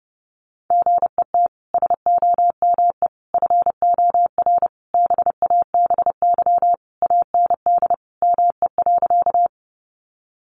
Morse code, heard as a keyed tone, about 30 wpm, 700 Hz. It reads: GET SOME FOR BABY AND ME.
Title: Sentences from Top 1300 Words 30wpm